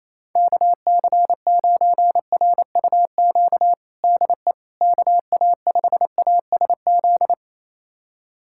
Morse code, heard as a keyed tone, about 28 wpm, 700 Hz. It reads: KC9RUQ DE KA5ASZ
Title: USA Callsign DE Callsign - 6 Characters - 28wpm